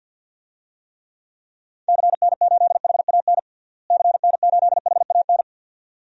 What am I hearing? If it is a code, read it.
Morse code, heard as a keyed tone, about 50 wpm, 700 Hz. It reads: CN8HAN CN8HAN
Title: DXCC-100 Call Signs - 6 Characters - Repeated 2x - 50wpm